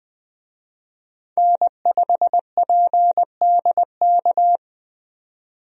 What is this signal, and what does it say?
Morse code, heard as a keyed tone, about 20 wpm, 700 Hz. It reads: N5PDK
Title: USA Call Signs - 5 Characters - 20wpm